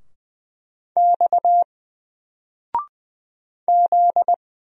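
Morse code, heard as a keyed tone, about 20 wpm, 700 Hz. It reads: X Z
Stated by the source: Single letter and number Morse drill at 20 wpm, from N4LCD.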